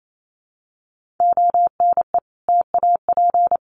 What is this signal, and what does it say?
Morse code, heard as a keyed tone, about 28 wpm, 700 Hz. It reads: ONE TAP
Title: Sentences from Top 1400 Words 28wpm